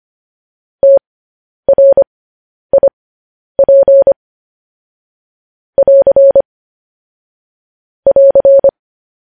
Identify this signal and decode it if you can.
Morse code, heard as a keyed tone, about 25 wpm, spaced as for 10 wpm, 550 Hz. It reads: TRIP <AR> <AR>
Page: news-headlines-in-morse-code-at-10-wpm